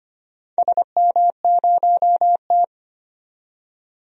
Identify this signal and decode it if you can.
Morse code, heard as a keyed tone, about 25 wpm, 700 Hz. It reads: SM0T
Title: DXCC-100 Call Signs - 4 Characters - 25wpm